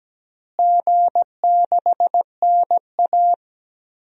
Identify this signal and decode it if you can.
Morse code, heard as a keyed tone, about 17 wpm, 700 Hz. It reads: G6NA
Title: DXCC-100 Call Signs - 4 Characters - Repeated 2x - 17wpm